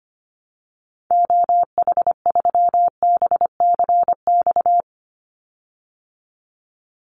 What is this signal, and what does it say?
Morse code, heard as a keyed tone, about 25 wpm, 700 Hz. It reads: OH3BCX
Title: Call Signs Repeated 3x - 25wpm